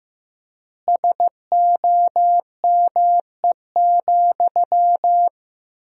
Morse code, heard as a keyed tone, about 15 wpm, 700 Hz. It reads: SOME,
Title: 4x Word-Spacing - Sentences from Top 200 Words - 15wpm